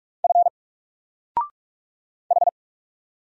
Morse code, heard as a keyed tone, about 44 wpm, 700 Hz. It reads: F H
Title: Single Character (Letter:Number:Punctuation:Prosign) 45wpm